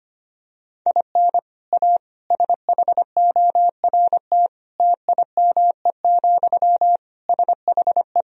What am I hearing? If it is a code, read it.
Morse code, heard as a keyed tone, about 25 wpm, 700 Hz. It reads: IN A SHORT TIME, SHE
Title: Sentences from Top 1000 Words - Review Part 2 - 25wpm